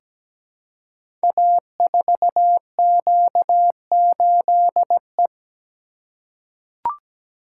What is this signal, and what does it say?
Morse code, heard as a keyed tone, about 17 wpm, 700 Hz. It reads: A4Q8E
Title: Code Groups - 5 Letter:Numbers - 17wpm